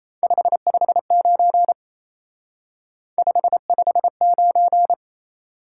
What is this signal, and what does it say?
Morse code, heard as a keyed tone, about 35 wpm, 700 Hz. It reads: 559 559
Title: Speed-Racing Top 180 QSO Elements 28wpm